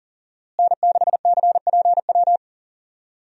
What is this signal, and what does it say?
Morse code, heard as a keyed tone, about 37 wpm, 700 Hz. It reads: N6CPW
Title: USA Call Signs - 5 Characters - Repeated 2x - 40wpm